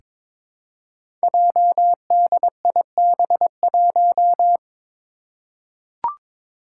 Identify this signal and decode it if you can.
Morse code, heard as a keyed tone, about 22 wpm, 700 Hz. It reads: JDIB1
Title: Five Challenging Letter-Number Combinations 22wpm